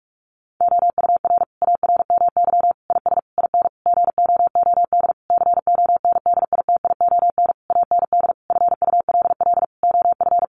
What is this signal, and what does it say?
Morse code, heard as a keyed tone, about 44 wpm, 700 Hz. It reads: OUR ARMY IS IN GOOD CONDITION AND FULL OF
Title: Sentences from Top 1500 Words 45wpm